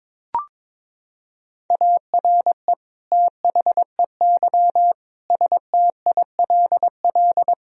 Morse code, heard as a keyed tone, about 22 wpm, 700 Hz. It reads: ARE THEY STILL
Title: Sentences from Top 1000 Words - Review Part 1 - 22wpm